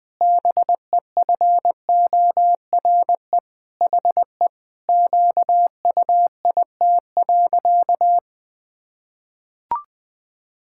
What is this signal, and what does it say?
Morse code, heard as a keyed tone, about 20 wpm, 700 Hz. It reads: BEFORE HE QUIT.
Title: Pangram Challenge v2 - 20wpm